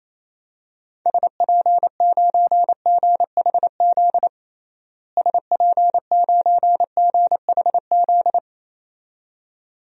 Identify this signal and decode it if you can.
Morse code, heard as a keyed tone, about 28 wpm, 700 Hz. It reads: SP9GHZ SP9GHZ
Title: DXCC-100 Call Signs - 6 Characters - Repeated 2x - 28wpm